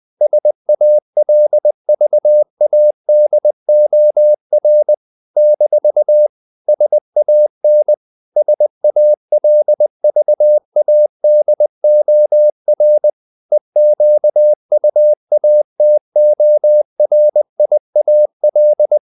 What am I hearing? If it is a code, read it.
Morse code, heard as a keyed tone, about 20 wpm, 600 Hz. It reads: SALVADOR - SAN SALVADOR EQUATORIAL
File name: CW / Practice / NationalCapitals20